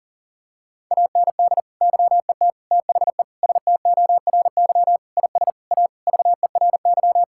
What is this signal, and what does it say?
Morse code, heard as a keyed tone, about 40 wpm, 690 Hz. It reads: AND YET THE STORY IS A VERY
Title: Sentences from Top 1000 Words - Review Part 6 - 40wpm